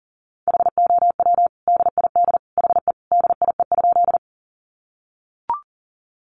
Morse code, heard as a keyed tone, about 37 wpm, 700 Hz. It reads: HOW DID HE DIE?
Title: Sentences from Top 1300 Words 40wpm